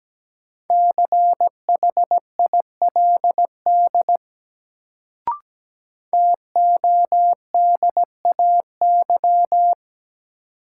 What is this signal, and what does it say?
Morse code, heard as a keyed tone, about 17 wpm, 700 Hz. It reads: CHILD TODAY
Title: Sets of 5 Words Encore - New Word Drill - 17wpm